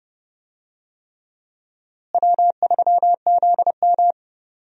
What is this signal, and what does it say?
Morse code, heard as a keyed tone, about 30 wpm, 700 Hz. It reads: W3ZM
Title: USA Call Signs - 4 Characters - Repeated 3x - 30wpm